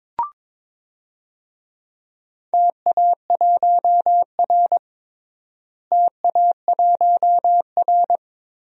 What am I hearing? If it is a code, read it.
Morse code, heard as a keyed tone, about 22 wpm, 700 Hz. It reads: TA1R TA1R
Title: DXCC-100 Call Signs - 4 Characters - Repeated 2x - 22wpm